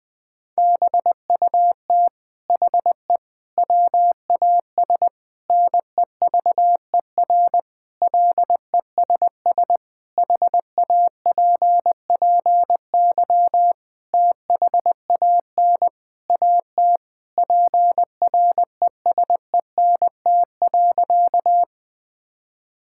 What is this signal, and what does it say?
Morse code, heard as a keyed tone, about 20 wpm, 700 Hz. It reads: BUT HE WAS NEVER LESS HAPPY THAN AT PRESENT.
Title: Sentences from Top 1000 Words - Review Part 5 - 20wpm